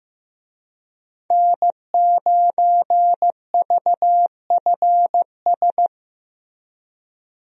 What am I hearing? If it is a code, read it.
Morse code, heard as a keyed tone, about 14 wpm, 700 Hz. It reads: N9VFS
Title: USA Call Signs - 5 Characters - Repeated 2x - 15wpm